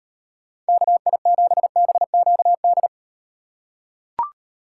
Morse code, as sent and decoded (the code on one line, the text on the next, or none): -.- .. --... -... --.- -..
KI7BQD